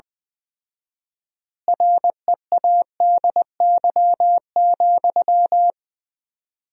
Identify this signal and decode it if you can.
Morse code, heard as a keyed tone, about 21 wpm, 700 Hz. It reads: READY,
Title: 4x Word-Spacing - Sentences from Top 500 Words - 20wpm